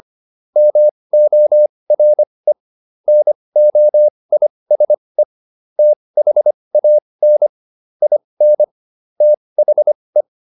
Morse code, heard as a keyed tone, about 25 wpm, spaced as for 20 wpm, 600 Hz. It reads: MORE NOISE THAN IN THE